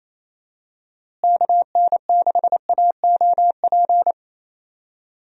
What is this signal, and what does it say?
Morse code, heard as a keyed tone, about 28 wpm, 700 Hz. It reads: KN6AOP
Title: USA Call Signs - 6 Characters - Repeated 2x - 28wpm